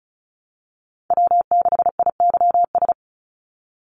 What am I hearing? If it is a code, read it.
Morse code, heard as a keyed tone, about 34 wpm, 700 Hz. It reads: W6IYS
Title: NA SOTA Chaser Callsigns 35wpm